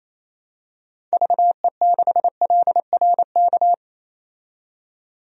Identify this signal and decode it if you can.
Morse code, heard as a keyed tone, about 28 wpm, 700 Hz. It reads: VE6LRK
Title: DXCC-100 Call Signs - 6 Characters - 28wpm